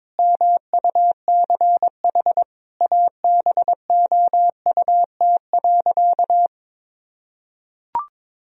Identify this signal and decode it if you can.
Morse code, heard as a keyed tone, about 22 wpm, 700 Hz. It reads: MUCH ABOUT.
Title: Sentences from Top 1000 Words - Review Part 2 - 22wpm